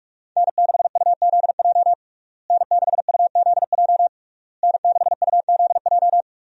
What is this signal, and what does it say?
Morse code, heard as a keyed tone, about 46 wpm, 700 Hz. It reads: N6UZJ N6UZJ N6UZJ
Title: USA Call Signs - 5 Characters - Repeated 3x - 45wpm